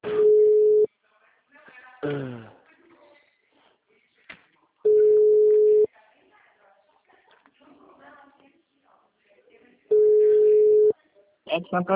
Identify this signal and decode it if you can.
Ringback tone (Europe, 425 Hz), with speech.